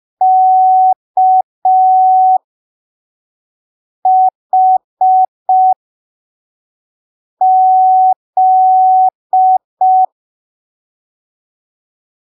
Morse code, heard as a keyed tone, about 5 wpm, spaced as for 3 wpm, 750 Hz. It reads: KHZ
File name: arquivos / texto / texto 12 05ppm